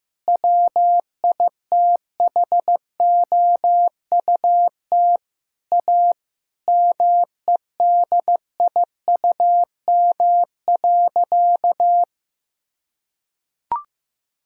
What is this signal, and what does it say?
Morse code, heard as a keyed tone, about 15 wpm, 700 Hz. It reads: WITHOUT A MEDIUM.
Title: Sentences from Top 1000 Words - Review Part 6 - 15wpm